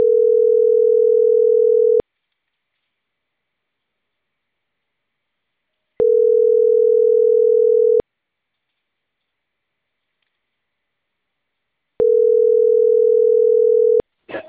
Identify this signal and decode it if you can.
Ringback tone (North America), 440 + 480 Hz.